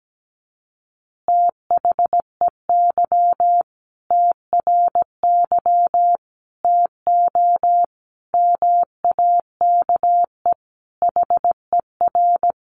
Morse code, heard as a keyed tone, about 17 wpm, 700 Hz. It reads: THEY TRY TO MAKE HER
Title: Easy Phrases 17wpm